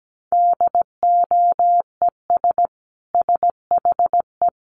Morse code, heard as a keyed tone, about 17 wpm, 700 Hz. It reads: DOES SHE
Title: Sentences from Top 400 Words 17wpm